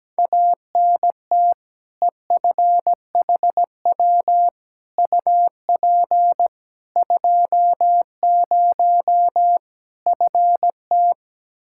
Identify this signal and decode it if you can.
Morse code, heard as a keyed tone, about 17 wpm, 700 Hz. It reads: ANT EFHW UP 20 FT
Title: Practice QSO - Exchange 2 - Info - 17wpm